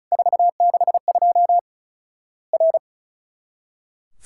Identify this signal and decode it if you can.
Morse code, heard as a keyed tone, about 34 wpm, 700 Hz. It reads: 462 R
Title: Worked All Europe DX Contest - QTC Exchange - 35wpm